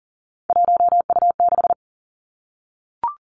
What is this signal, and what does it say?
Morse code, heard as a keyed tone, about 42 wpm, 690 Hz. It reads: 1U6
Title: Three Challenging Letter-Number Combinations 40wpm